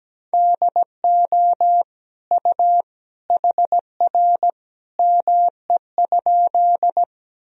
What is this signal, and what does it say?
Morse code, heard as a keyed tone, about 17 wpm, 700 Hz. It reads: DO U HR ME?
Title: Sentences from Top 300 Words and Abbreviations 17wpm